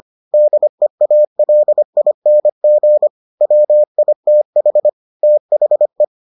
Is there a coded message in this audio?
Yes: Morse code from 0.3 to 6.0 s, heard as a keyed tone, about 25 wpm, 600 Hz. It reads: DEALING WITH THE